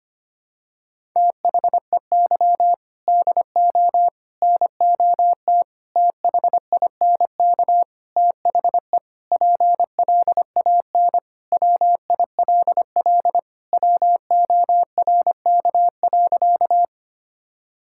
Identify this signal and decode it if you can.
Morse code, heard as a keyed tone, about 25 wpm, 700 Hz. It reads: THEY DO NOT THINK THE PLAN WILL WORK.
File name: practice / 25wpm / Sentences with 4-Word Phrases - 25wpm